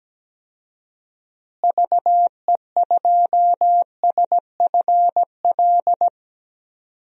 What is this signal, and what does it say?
Morse code, heard as a keyed tone, about 17 wpm, 700 Hz. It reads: VE2SFL
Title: DXCC-100 Call Signs - 6 Characters - Repeated 3x - 17wpm